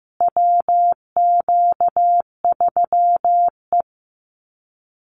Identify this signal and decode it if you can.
Morse code, heard as a keyed tone, about 15 wpm, 700 Hz. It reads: WQ3E
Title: Call Signs Repeated 2x - 15wpm